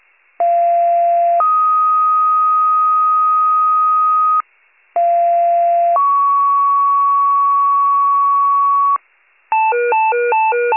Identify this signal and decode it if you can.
Two-tone page, heard as a one-second tone, then a three-second tone, about 690 then 1250 Hz, about 690 then 1120 Hz. It is heard from 0.4 to 9.0 s.